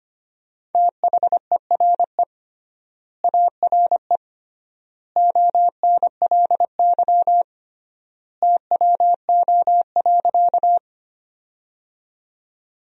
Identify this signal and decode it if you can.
Morse code, heard as a keyed tone, about 25 wpm, 700 Hz. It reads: THERE ARE ONLY TWO.
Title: 3x Word-Spacing - Sentences from Top 200 Words - 25wpm